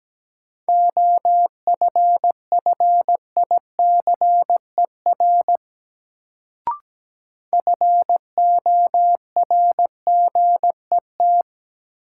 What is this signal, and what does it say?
Morse code, heard as a keyed tone, about 17 wpm, 700 Hz. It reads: OFFICER FORGET
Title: Sentences from Top 1200 Words - New Word Drill Alt - 17wpm